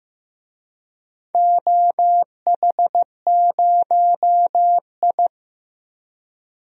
Morse code, heard as a keyed tone, about 15 wpm, 700 Hz. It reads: OH0I
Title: DXCC-100 Call Signs - 4 Characters - 15wpm